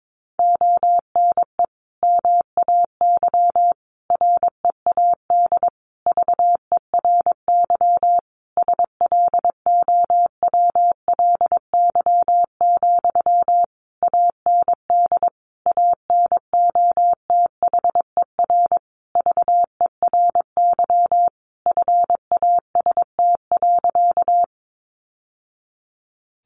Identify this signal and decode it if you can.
Morse code, heard as a keyed tone, about 22 wpm, 700 Hz. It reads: ONE MAY READ VERY SLOWLY, AND ANOTHER VERY FAST.
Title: Sentences from Top 1100 Words 22wpm